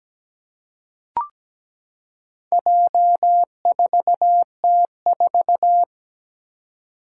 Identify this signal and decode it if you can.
Morse code, heard as a keyed tone, about 17 wpm, 700 Hz. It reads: J4T4
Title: Four Challenging Letter-Number Combinations 17wpm